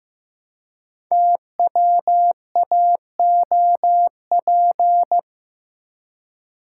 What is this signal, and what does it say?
Morse code, heard as a keyed tone, about 15 wpm, 700 Hz. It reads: TWAOP
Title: Code Groups - 5 Letters - 15wpm